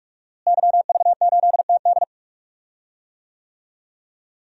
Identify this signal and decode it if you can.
Morse code, heard as a keyed tone, about 45 wpm, 700 Hz. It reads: YV8TD
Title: DXCC-100 Call Signs - 5 Characters - Repeated 2x - 45wpm